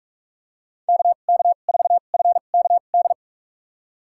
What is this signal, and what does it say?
Morse code, heard as a keyed tone, about 41 wpm, spaced as for 31 wpm, 700 Hz. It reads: KK4FKD
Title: USA Call Signs - 6 Characters - 2x Character Spacing - 45wpm